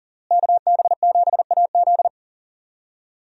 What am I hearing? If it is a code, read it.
Morse code, heard as a keyed tone, about 37 wpm, 700 Hz. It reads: KB7AZ
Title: USA Call Signs - 5 Characters - Repeated 3x - 40wpm